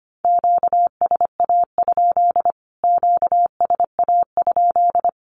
Morse code, heard as a keyed tone, about 25 wpm, 700 Hz. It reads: QSA? QSA?
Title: Q-Codes Repeated 3x 25wpm